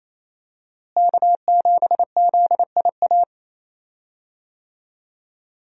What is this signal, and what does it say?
Morse code, heard as a keyed tone, about 28 wpm, 700 Hz. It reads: K7ZIA
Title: USA Call Signs - 5 Characters - Repeated 2x - 28wpm